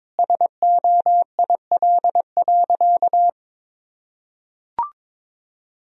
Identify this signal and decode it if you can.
Morse code, heard as a keyed tone, about 22 wpm, 700 Hz. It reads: SOIL.
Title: Sentences from Top 1000 Words - Review Part 4 - 22wpm